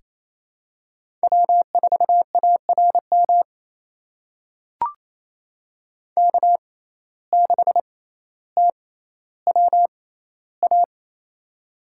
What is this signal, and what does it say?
Morse code, heard as a keyed tone, about 28 wpm, spaced as for 14 wpm, 700 Hz. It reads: W4ARM K6TWA
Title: USA Call Signs - 5 Characters - 6x Character Spacing - 28wpm